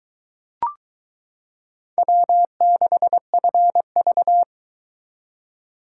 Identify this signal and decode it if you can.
Morse code, heard as a keyed tone, about 23 wpm, 700 Hz. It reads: W6FV